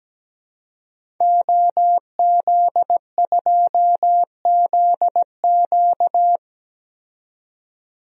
Morse code, heard as a keyed tone, about 17 wpm, 700 Hz. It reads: OZ2ZQ
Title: DXCC-100 Call Signs - 5 Characters - 17wpm